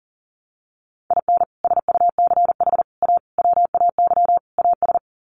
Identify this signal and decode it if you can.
Morse code, heard as a keyed tone, about 39 wpm, 700 Hz. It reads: IN SUCH A WAY AS
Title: Sets of 5 Words Encore 40wpm